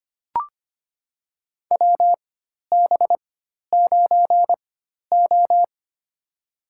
Morse code, heard as a keyed tone, about 25 wpm, spaced as for 12 wpm, 700 Hz. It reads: WB9O
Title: USA Call Signs - 4 Characters - 4x Character Spacing - 25wpm